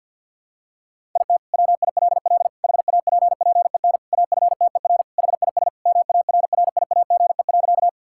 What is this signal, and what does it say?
Morse code, heard as a keyed tone, about 52 wpm, 700 Hz. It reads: IT WILL HAPPEN AFTER HIS MARRIAGE.
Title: Sentences from Top 1000 Words - Review Part 5 - 50wpm